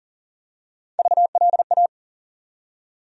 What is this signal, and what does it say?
Morse code, heard as a keyed tone, about 40 wpm, 700 Hz. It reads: VLA
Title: USA Call Sign Postfixes - 3 Letters 40wpm